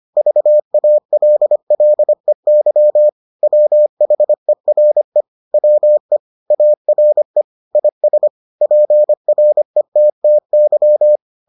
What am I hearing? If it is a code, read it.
Morse code, heard as a keyed tone, about 25 wpm, 600 Hz. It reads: VALLEY WHERE WE ARE IS PRETTY